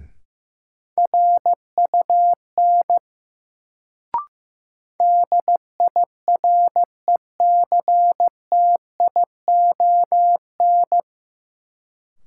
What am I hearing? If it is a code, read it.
Morse code, heard as a keyed tone, about 15 wpm, 700 Hz. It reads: RUN DIRECTION
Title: Sets of 4 Words - New Word Drill - 15wpm